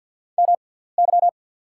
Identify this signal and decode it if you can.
Morse code, heard as a keyed tone, about 48 wpm, spaced as for 17 wpm, 700 Hz. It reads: MY
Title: 6x Character Spacing - Top 100 Words - 50wpm